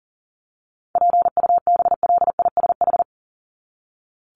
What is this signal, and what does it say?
Morse code, heard as a keyed tone, about 40 wpm, 700 Hz. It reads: PUBLISH
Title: ED Suffix - Top 100 Words with Base Words 40wpm